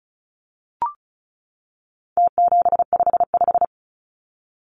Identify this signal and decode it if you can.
Morse code, heard as a keyed tone, about 34 wpm, 700 Hz. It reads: T755
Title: Numbers - Contest Style - 4-Digit 35wpm